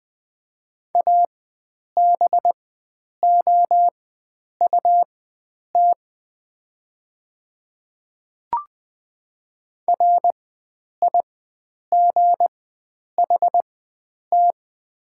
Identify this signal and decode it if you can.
Morse code, heard as a keyed tone, about 20 wpm, spaced as for 9 wpm, 700 Hz. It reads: ABOUT RIGHT
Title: 4x Character Spacing - Top 500 Words - 20wpm